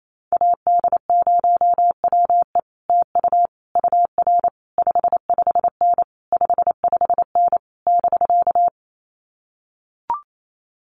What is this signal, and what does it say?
Morse code, heard as a keyed tone, about 28 wpm, 700 Hz. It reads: AD0WE TU UR 55N 55N <BK>
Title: Common QSO Phrases Encore 28wpm